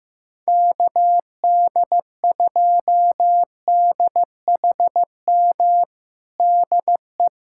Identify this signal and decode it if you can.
Morse code, heard as a keyed tone, about 15 wpm, 700 Hz. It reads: KD2DHM DE
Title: USA Callsign DE Callsign - 6 Characters - 15wpm